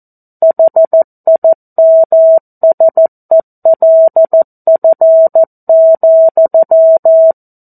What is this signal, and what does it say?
Morse code, heard as a keyed tone, about 14 wpm, 650 Hz. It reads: HIMSELF,
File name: 02-Chapter-02 14 WPM